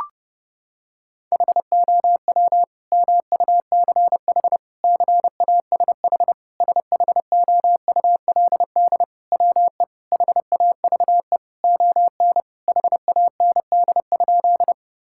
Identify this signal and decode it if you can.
Morse code, heard as a keyed tone, about 30 wpm, 700 Hz. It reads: HOW MUCH CASH SHOULD WE HAVE ON HAND?
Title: Sentences from Top 1000 Words - Review Part 5 - 30wpm